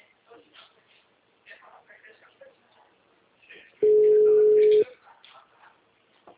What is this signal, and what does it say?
Ringback tone (Europe, 425 Hz), with speech.
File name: out- -2018- - - .47953